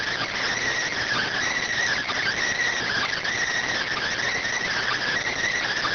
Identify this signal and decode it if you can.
Emergency Alert System (SAME) tones, heard as harsh buzzing bursts. It is heard from 1.1 to 4.9 s.